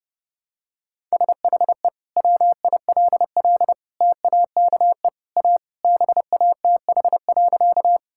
Morse code, heard as a keyed tone, about 32 wpm, 700 Hz. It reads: SHE WILL TAKE A BATH.
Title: Sentences from Top 1600 Words 30wpm